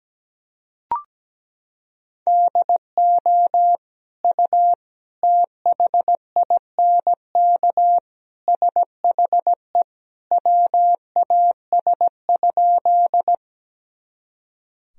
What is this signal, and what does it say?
Morse code, heard as a keyed tone, about 17 wpm, 700 Hz. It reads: DO U THINK SHE WAS?
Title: Sentences from Top 200 Words and Abbreviations 17wpm